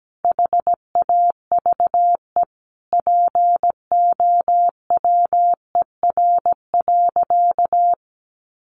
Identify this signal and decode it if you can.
Morse code, heard as a keyed tone, about 17 wpm, 700 Hz. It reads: HAVE POWER.
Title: Sentences from Top 1100 Words 17wpm